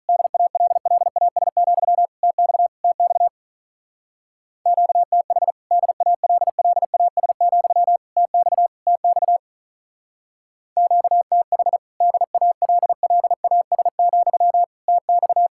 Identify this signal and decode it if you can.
Morse code, heard as a keyed tone, about 48 wpm, 700 Hz. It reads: DALLAS, TX TX QTH DALLAS, TX TX QTH DALLAS, TX